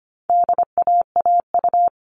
Morse code, heard as a keyed tone, about 25 wpm, 700 Hz. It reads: DAAU
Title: All Cut Numbers - 4-Digit 25wpm